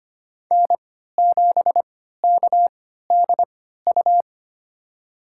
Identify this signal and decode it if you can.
Morse code, heard as a keyed tone, about 25 wpm, spaced as for 14 wpm, 700 Hz. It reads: N7KDU
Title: USA Call Signs - 5 Characters - 3x Character Spacing - 25wpm